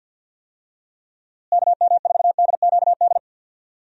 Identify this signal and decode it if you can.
Morse code, heard as a keyed tone, about 45 wpm, 700 Hz. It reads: KM4DQD
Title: USA Call Signs - 6 Characters - Repeated 3x - 50wpm